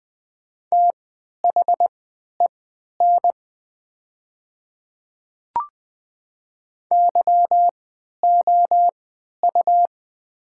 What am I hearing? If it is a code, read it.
Morse code, heard as a keyed tone, about 20 wpm, spaced as for 11 wpm, 700 Hz. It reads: THEN YOU
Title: 3x Character Spacing - Top 100 Words - 20wpm